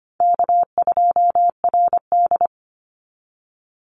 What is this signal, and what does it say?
Morse code, heard as a keyed tone, about 25 wpm, 700 Hz. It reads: K2RD
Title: Call Signs Repeated 3x - 25wpm